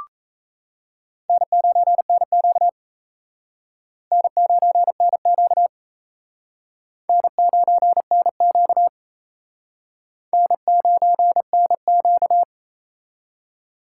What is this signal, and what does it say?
Morse code, heard as a keyed tone, about 36 wpm, 700 Hz. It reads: N9NQ N9NQ N9NQ N9NQ